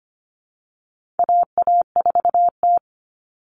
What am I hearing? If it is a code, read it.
Morse code, heard as a keyed tone, about 25 wpm, 700 Hz. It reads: AA4T
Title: All Cut Numbers - 4-Digit 25wpm